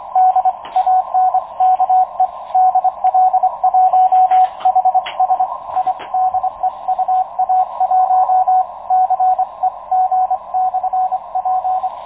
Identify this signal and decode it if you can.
Morse code, heard as a keyed tone, about 25 wpm, spaced as for 22 wpm, 740 Hz, with radio noise. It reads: DANKE DL1HSI DE UA1CEG/P
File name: Expedition 2019 / ua1ceg-p 10.22.2019 09.36 utc audio DL1HSI